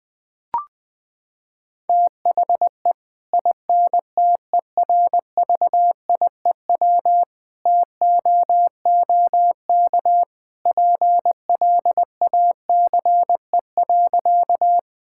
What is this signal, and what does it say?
Morse code, heard as a keyed tone, about 21 wpm, 700 Hz. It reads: THE INTERVIEW TOOK PLACE.
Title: Sentences from Top 1600 Words 20wpm